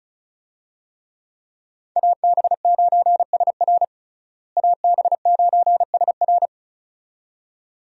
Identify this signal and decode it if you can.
Morse code, heard as a keyed tone, about 36 wpm, 700 Hz. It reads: AB9SR AB9SR
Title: USA Call Signs - 5 Characters - Repeated 2x - 35wpm